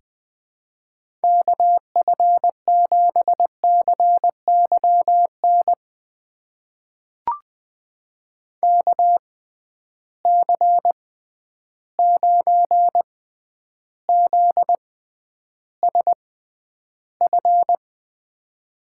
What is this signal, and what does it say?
Morse code, heard as a keyed tone, about 20 wpm, spaced as for 10 wpm, 700 Hz. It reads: KF7CYN KC9ZSF
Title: USA Call Signs - 6 Characters - 6x Character Spacing - 20wpm